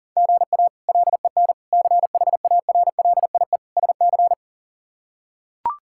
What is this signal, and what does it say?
Morse code, heard as a keyed tone, about 40 wpm, 700 Hz. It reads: GA LEN CHARLIE SC
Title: K1USN SST QSO Practice - 40wpm